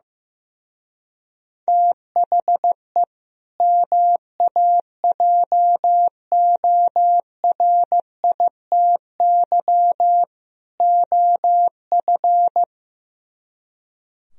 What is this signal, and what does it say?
Morse code, heard as a keyed tone, about 16 wpm, 700 Hz. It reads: THE MAJORITY OF
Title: Speed-Racing - Sets of 3 Words 15wpm